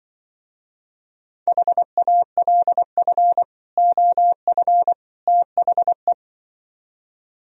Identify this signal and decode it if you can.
Morse code, heard as a keyed tone, about 24 wpm, 700 Hz. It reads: HALF OF THE
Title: Speed-Racing - Sets of 3 Words 20wpm